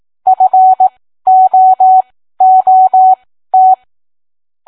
Morse code, heard as a keyed tone, about 18 wpm, spaced as for 13 wpm, 750 Hz. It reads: FOOT